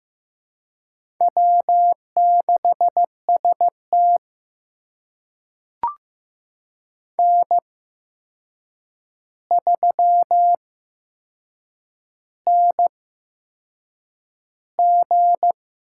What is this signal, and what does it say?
Morse code, heard as a keyed tone, about 15 wpm, spaced as for 6 wpm, 700 Hz. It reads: W6ST N3NG
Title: USA Call Signs - 4 Characters - 8x Character Spacing - 15wpm